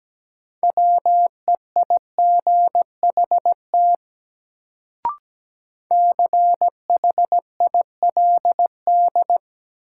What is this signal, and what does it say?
Morse code, heard as a keyed tone, about 17 wpm, 700 Hz. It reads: WEIGHT CHILD
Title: Sentences from Top 800 Words - New Word Drill - 17wpm